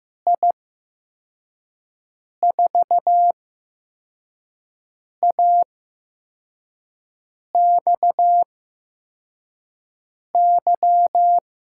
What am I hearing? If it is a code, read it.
Morse code, heard as a keyed tone, about 15 wpm, spaced as for 4 wpm, 700 Hz. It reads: I4AXY
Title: USA Call Signs - 6 Characters - 8x Character Spacing - 15wpm